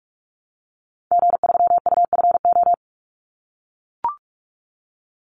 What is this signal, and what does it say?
Morse code, heard as a keyed tone, about 44 wpm, 700 Hz. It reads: G3UFO
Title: Call Signs - 45wpm